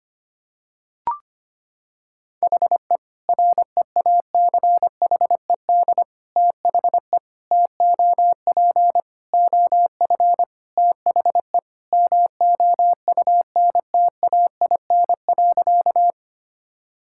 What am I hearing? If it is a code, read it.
Morse code, heard as a keyed tone, about 25 wpm, 700 Hz. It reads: HE REACHED THE TOP OF THE MOUNTAIN.
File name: practice / 25wpm / Sentences with 3-Word Phrases - 25wpm